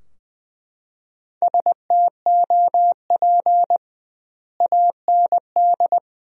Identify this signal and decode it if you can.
Morse code, heard as a keyed tone, about 20 wpm, 700 Hz. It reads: STOP AND
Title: 2x Word-Spacing - Sentences from Top 300 Words - 20wpm